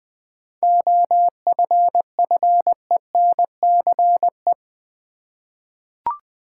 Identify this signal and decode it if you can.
Morse code, heard as a keyed tone, about 20 wpm, 700 Hz. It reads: OFFENCE
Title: 7 Letter Words - First 3 Letters Given - 20wpm